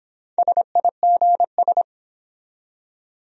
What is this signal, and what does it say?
Morse code, heard as a keyed tone, about 24 wpm, 700 Hz. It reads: SIGS